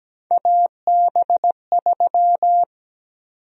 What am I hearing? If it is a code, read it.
Morse code, heard as a keyed tone, about 17 wpm, 700 Hz. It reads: AB3
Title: USA Call Sign Prefixes - 17wpm